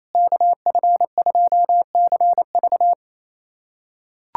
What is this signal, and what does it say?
Morse code, heard as a keyed tone, about 27 wpm, 700 Hz. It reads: KF2CV
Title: USA Callsign DE Callsign - 5 Characters - 28wpm